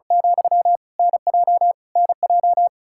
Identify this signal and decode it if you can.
Morse code, heard as a keyed tone, about 36 wpm, 700 Hz. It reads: , NJ NJ
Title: Practice QSO - Exchange 1 - Beginning - 35wpm